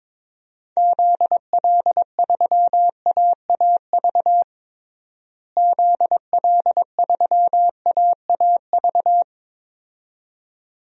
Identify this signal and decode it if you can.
Morse code, heard as a keyed tone, about 22 wpm, 700 Hz. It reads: ZL3AAV ZL3AAV
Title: DXCC-100 Call Signs - 6 Characters - Repeated 2x - 22wpm